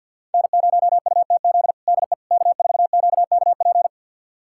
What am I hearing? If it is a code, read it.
Morse code, heard as a keyed tone, about 51 wpm, 700 Hz. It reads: N0UTZ DE K4QKP
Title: USA Callsign DE Callsign - 5 Characters - 50wpm